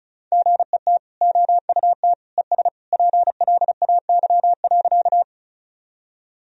Morse code, heard as a keyed tone, about 34 wpm, 700 Hz. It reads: GET OUT ES PLAY.
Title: Sentences from Top 200 Words and Abbreviations 35wpm